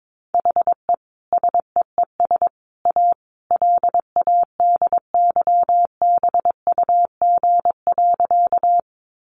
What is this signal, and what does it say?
Morse code, heard as a keyed tone, about 22 wpm, 700 Hz. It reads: HE SEES A LADYBUG.
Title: Additional Easy Phrases 22wpm